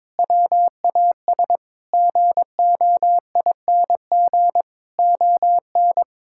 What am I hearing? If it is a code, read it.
Morse code, heard as a keyed tone, about 21 wpm, 700 Hz. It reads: WAS GOING ON
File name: practice / 22wpm / Sets of 8 Words - 22wpm